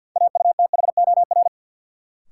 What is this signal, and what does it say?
Morse code, heard as a keyed tone, about 51 wpm, 700 Hz. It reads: AUTHOR
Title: Sets of 4 Words - New Word Drill - 50wpm